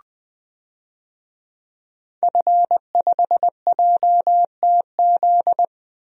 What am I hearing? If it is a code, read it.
Morse code, heard as a keyed tone, about 20 wpm, 700 Hz. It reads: F5JTZ
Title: DXCC-100 Call Signs - 5 Characters - Repeated 2x - 20wpm